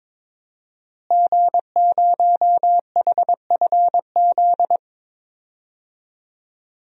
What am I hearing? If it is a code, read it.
Morse code, heard as a keyed tone, about 22 wpm, 700 Hz. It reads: G0HFZ
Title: DXCC-100 Call Signs - 5 Characters - Repeated 3x - 22wpm